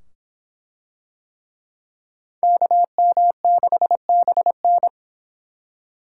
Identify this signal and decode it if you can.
Morse code, heard as a keyed tone, about 24 wpm, 700 Hz. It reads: KM6BN